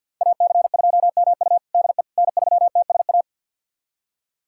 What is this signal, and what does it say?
Morse code, heard as a keyed tone, about 49 wpm, 690 Hz. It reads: AC2MU DE N3TSA
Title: USA Callsign DE Callsign - 5 Characters - 50wpm